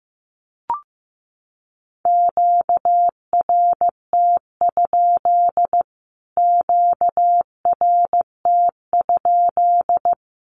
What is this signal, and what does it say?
Morse code, heard as a keyed tone, about 14 wpm, 700 Hz. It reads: QRT? QRT?
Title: Q-Codes Repeated 3x 15wpm